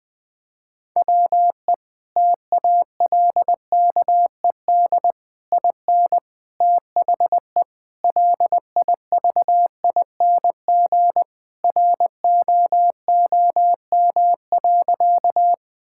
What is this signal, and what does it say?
Morse code, heard as a keyed tone, about 19 wpm, 700 Hz. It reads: WE TALKED IN THE LIVING ROOM.
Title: Sentences with 4-Word Phrases - 20wpm